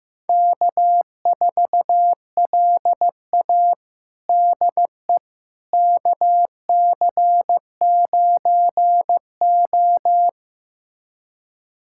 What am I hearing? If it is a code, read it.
Morse code, heard as a keyed tone, about 15 wpm, 700 Hz. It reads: K4LA DE KC9O
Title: USA Callsign DE Callsign - 4 Characters - 15wpm